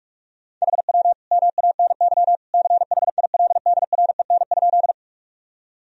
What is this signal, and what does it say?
Morse code, heard as a keyed tone, about 41 wpm, 700 Hz. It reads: HW MANY CHILDREN?
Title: Sentences from Top 300 Words and Abbreviations 45wpm